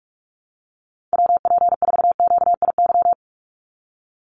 Morse code, heard as a keyed tone, about 44 wpm, 700 Hz. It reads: WP4QIY
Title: USA Call Signs 45wpm